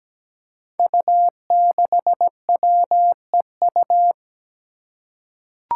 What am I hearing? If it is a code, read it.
Morse code, heard as a keyed tone, about 17 wpm, 700 Hz. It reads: U6WEU
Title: Five Challenging Letter-Number Combinations 17wpm